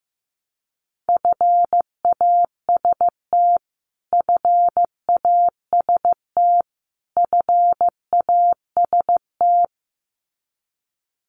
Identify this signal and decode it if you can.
Morse code, heard as a keyed tone, about 15 wpm, 700 Hz. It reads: FAST FAST FAST
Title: Top 500 Words Repeated 15wpm